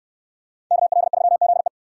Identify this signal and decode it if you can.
Morse code, heard as a keyed tone, about 68 wpm, 690 Hz. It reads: KL3ZE